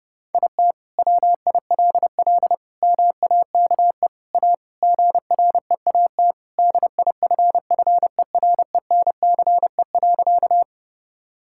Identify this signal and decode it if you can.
Morse code, heard as a keyed tone, about 32 wpm, 700 Hz. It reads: IT WILL MAKE A GREAT DIFFERENCE.
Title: Sentences from Top 1000 Words - Review Part 5 - 30wpm